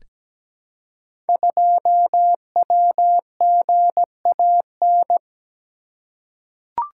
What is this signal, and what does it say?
Morse code, heard as a keyed tone, about 17 wpm, 700 Hz. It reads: 2WGAN
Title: Code Groups - 5 Letter:Numbers - 17wpm